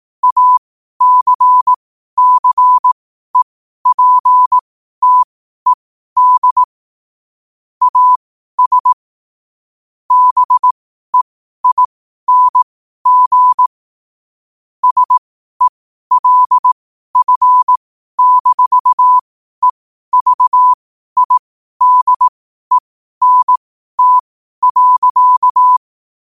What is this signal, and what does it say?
Morse code, heard as a keyed tone, about 18 wpm, spaced as for 12 wpm, 1000 Hz. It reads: ACCEPTED AS BEING SELF-EVIDENT.
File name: qotdfiles / 12wpm